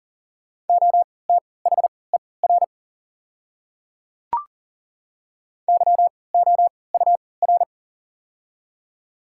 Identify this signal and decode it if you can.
Morse code, heard as a keyed tone, about 40 wpm, spaced as for 23 wpm, 700 Hz. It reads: OTHER YOUR
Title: 3x Character Spacing - Top 100 Words - 40wpm